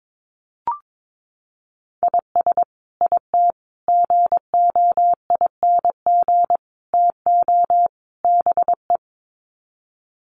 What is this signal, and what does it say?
Morse code, heard as a keyed tone, about 22 wpm, 700 Hz. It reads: IS IT GOING TO BE
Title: Sets of 5 Words Encore2 22wpm